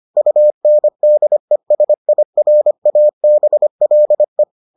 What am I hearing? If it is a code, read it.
Morse code, heard as a keyed tone, about 25 wpm, 600 Hz. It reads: UNDESIRABLE